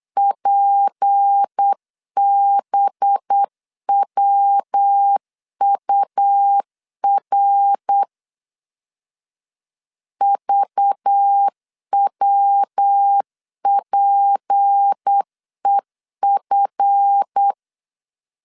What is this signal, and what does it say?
Morse code, heard as a keyed tone, about 8 wpm, 800 Hz. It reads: PBWUR VWPEF
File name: CW / DET / 1 9 1500C S B 8wpm 5624